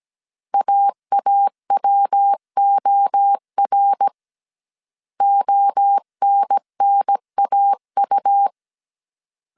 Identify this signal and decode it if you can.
Morse code, heard as a keyed tone, about 17 wpm, 800 Hz. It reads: AAWOR ONNAU
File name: CW / DET / 1 3 1500C S A 16wpm 2141